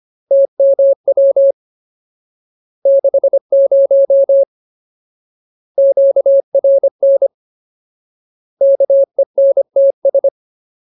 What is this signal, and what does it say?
Morse code, heard as a keyed tone, about 25 wpm, 550 Hz. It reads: TMW 60 QRN KENTS
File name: Ultimate Word List / 25WPM Ultimate